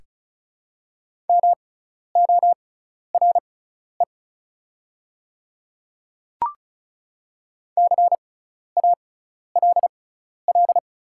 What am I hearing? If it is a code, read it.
Morse code, heard as a keyed tone, about 36 wpm, spaced as for 12 wpm, 700 Hz. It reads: MORE CALL
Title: 6x Character Spacing - Top 100 Words - 35wpm